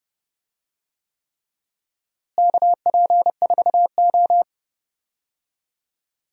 Morse code, heard as a keyed tone, about 30 wpm, 700 Hz. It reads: KP4O
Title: DXCC-100 Call Signs - 4 Characters - Repeated 3x - 30wpm